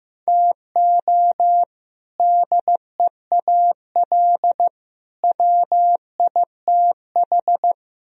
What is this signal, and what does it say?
Morse code, heard as a keyed tone, about 15 wpm, 700 Hz. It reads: TO DEAL WITH
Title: Speed-Racing - Sets of 3 Words 15wpm